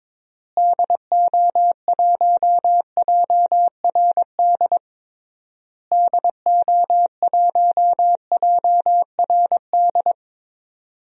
Morse code, heard as a keyed tone, about 22 wpm, 700 Hz. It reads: DO1JRD DO1JRD
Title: DXCC-100 Call Signs - 6 Characters - Repeated 3x - 22wpm